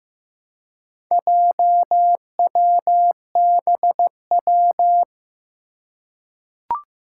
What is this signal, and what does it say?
Morse code, heard as a keyed tone, about 15 wpm, 700 Hz. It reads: JWBW
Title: Four Challenging Letter-Number Combinations 15wpm